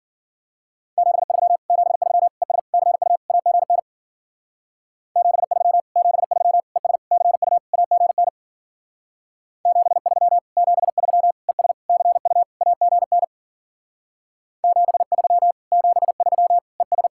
Morse code, heard as a keyed tone, about 50 wpm, 700 Hz. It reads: 73 73 ES CU AGN 73 73 ES CU AGN 73 73 ES CU AGN 73 73 ES
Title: Speed-Racing Common QSO Phrases - 40wpm